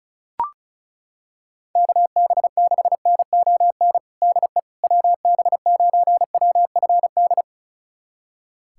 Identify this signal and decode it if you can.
Morse code, heard as a keyed tone, about 36 wpm, 700 Hz. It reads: KB6NON DE WB9WFD
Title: USA Callsign DE Callsign 35wpm